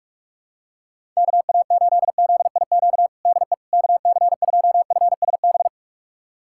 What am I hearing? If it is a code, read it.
Morse code, heard as a keyed tone, about 41 wpm, 700 Hz. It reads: KA8ZIQ DE KC2FSB
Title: USA Callsign DE Callsign 45wpm